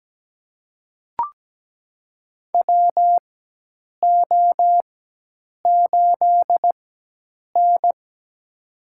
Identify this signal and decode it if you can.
Morse code, heard as a keyed tone, about 17 wpm, spaced as for 8 wpm, 700 Hz. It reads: WO8N
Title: USA Call Signs - 4 Characters - 4x Character Spacing - 17wpm